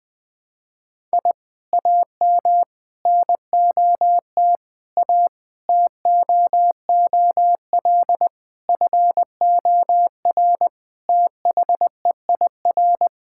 Morse code, heard as a keyed tone, about 20 wpm, 700 Hz. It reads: I AM NOT A TOOL FOR THEIR
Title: Sentences from Top 1000 Words - Review Part 6 - 20wpm